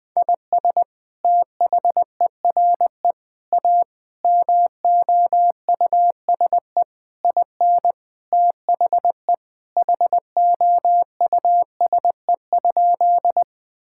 Morse code, heard as a keyed tone, about 21 wpm, 700 Hz. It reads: IS THERE A MOUSE IN THE HOUSE?
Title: Sentences from Top 1000 Words - Review Part 5 - 20wpm